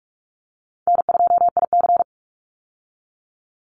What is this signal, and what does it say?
Morse code, heard as a keyed tone, about 46 wpm, 700 Hz. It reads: N2IC
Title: Easy Call Signs - 45wpm